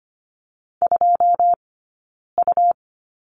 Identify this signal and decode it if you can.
Morse code, heard as a keyed tone, about 25 wpm, 700 Hz. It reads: 2 U
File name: Lesson 17d - Letter u - Two Characters 25wpm